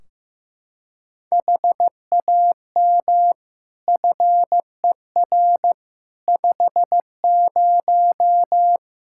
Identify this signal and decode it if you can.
Morse code, heard as a keyed tone, about 15 wpm, 700 Hz. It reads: HAM FER 50
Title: Practice QSO - Exchange 3 - More Info - 15wpm